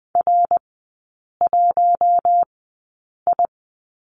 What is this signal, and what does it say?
Morse code, heard as a keyed tone, about 20 wpm, spaced as for 8 wpm, 700 Hz. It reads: R1I
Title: 178 Lesson 29e - Slash - Three Characters 20wpm